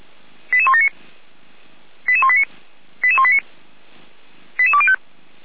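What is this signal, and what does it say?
Five-tone selective call (ZVEI), heard as quick runs of five tones, tones of about 70 ms. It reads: 88179 88189 88189 88205
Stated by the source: ZVEI five-tone selective call received via SDR, from Feuerwehr Orlamünde (fire brigade) audio server.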